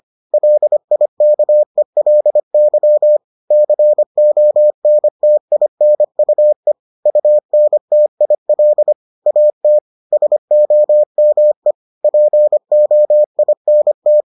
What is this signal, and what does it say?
Morse code, heard as a keyed tone, about 25 wpm, 600 Hz. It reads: LIKELY CONTINUE UNTIL AT SOME POINT